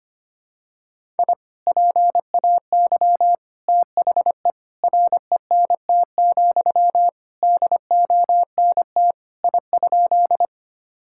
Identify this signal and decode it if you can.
Morse code, heard as a keyed tone, about 25 wpm, 700 Hz. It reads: I PAY THE RENT, DONT I?
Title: Sentences from Top 1000 Words - Review Part 5 - 25wpm